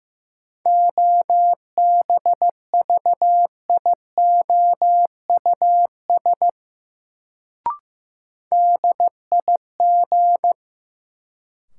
Morse code, heard as a keyed tone, about 15 wpm, 700 Hz. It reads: OBVIOUS DIG
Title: Sentences from Top 1600 Words - New Word Drill Alt - 15wpm